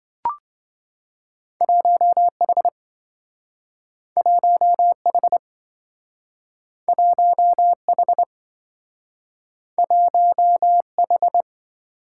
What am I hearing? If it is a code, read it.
Morse code, heard as a keyed tone, about 27 wpm, 700 Hz. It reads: 1H 1H 1H 1H